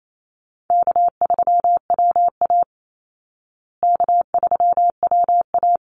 Morse code, heard as a keyed tone, about 28 wpm, 700 Hz. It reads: K3WA K3WA
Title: Call Signs Repeated 3x - 28wpm